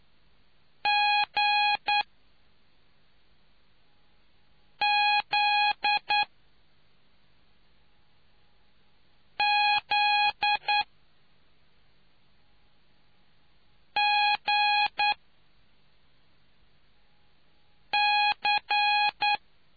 Morse code, heard as a keyed tone, about 9 wpm, spaced as for 3 wpm, 830 Hz. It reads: GZZGC